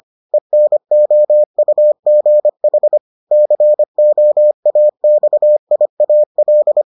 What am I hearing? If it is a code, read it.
Morse code, heard as a keyed tone, about 25 wpm, 600 Hz. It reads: ENOUGH COAXIAL